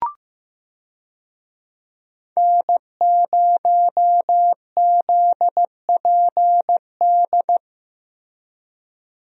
Morse code, heard as a keyed tone, about 15 wpm, 700 Hz. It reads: N0ZPD
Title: USA Call Signs - 5 Characters - Repeated 3x - 15wpm